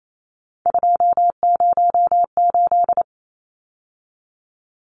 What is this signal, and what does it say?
Morse code, heard as a keyed tone, about 28 wpm, 700 Hz. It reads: 208
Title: Numbers - 3-Digit 28wpm